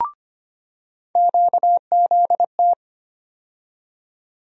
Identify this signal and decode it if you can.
Morse code, heard as a keyed tone, about 25 wpm, 700 Hz. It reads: QZT
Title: USA Call Sign Postfixes - 3 Letters 25wpm